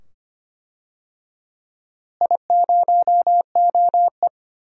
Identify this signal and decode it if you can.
Morse code, heard as a keyed tone, about 25 wpm, 700 Hz. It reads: I0OE
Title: DXCC-100 Call Signs - 4 Characters - Repeated 2x - 25wpm